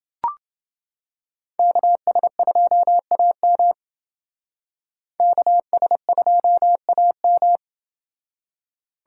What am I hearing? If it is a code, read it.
Morse code, heard as a keyed tone, about 29 wpm, 700 Hz. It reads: KS2AM KS2AM